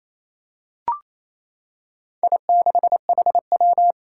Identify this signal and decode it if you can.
Morse code, heard as a keyed tone, about 27 wpm, 700 Hz. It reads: I6HW
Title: Four Challenging Letter-Number Combinations 28wpm